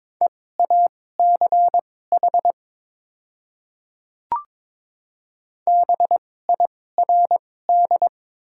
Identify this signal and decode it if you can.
Morse code, heard as a keyed tone, about 22 wpm, spaced as for 16 wpm, 700 Hz. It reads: EACH BIRD
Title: 2x Character Spacing - Top 500 Words - 22wpm